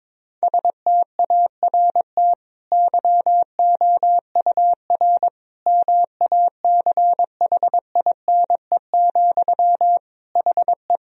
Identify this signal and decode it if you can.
Morse code, heard as a keyed tone, about 22 wpm, 700 Hz. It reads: START YOUR MACHINE, HE
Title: Sentences from Top 1000 Words - Review Part 2 - 22wpm